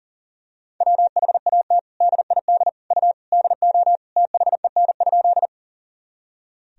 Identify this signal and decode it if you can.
Morse code, heard as a keyed tone, about 40 wpm, 700 Hz. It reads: WHAT DID U DO THEN?
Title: Sentences from Top 100 Words and Abbreviations 40wpm